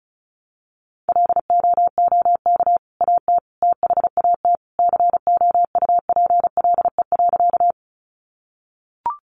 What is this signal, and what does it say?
Morse code, heard as a keyed tone, about 34 wpm, 700 Hz. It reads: LOOK AT THAT COUPLE.
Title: Sentences from Top 1100 Words 35wpm